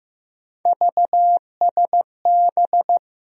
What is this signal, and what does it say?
Morse code, heard as a keyed tone, about 15 wpm, 700 Hz. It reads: VSB
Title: USA Call Sign Postfixes - 3 Letters 15wpm